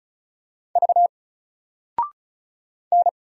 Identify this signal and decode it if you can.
Morse code, heard as a keyed tone, about 36 wpm, 700 Hz. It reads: V N
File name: Single-Letter-Number-35wpm